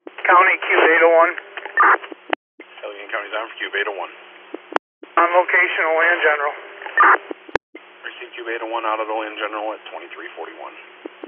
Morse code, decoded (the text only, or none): none